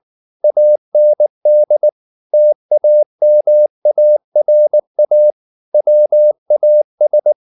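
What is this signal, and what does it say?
Morse code, heard as a keyed tone, about 19 wpm, 600 Hz. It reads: AND TAMARA WAS
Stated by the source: Morse code at 19 wpm from CW med Gnister, reading Ken Follett's novel Never in English.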